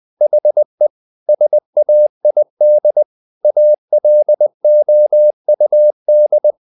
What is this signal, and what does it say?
Morse code, heard as a keyed tone, about 20 wpm, 600 Hz. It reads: HE SAID ALOUD
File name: the old man and the sea / the old man and the sea part 1 20wpm